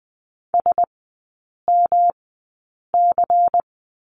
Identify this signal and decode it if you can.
Morse code, heard as a keyed tone, about 20 wpm, spaced as for 8 wpm, 700 Hz. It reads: SMC
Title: 121 Lesson 20e - Number 3 - Three Characters 20wpm